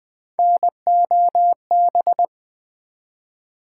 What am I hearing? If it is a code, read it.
Morse code, heard as a keyed tone, about 20 wpm, 700 Hz. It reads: NOB
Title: Build A Word - 3 Letters At A Time - 6-Letter Words - 20wpm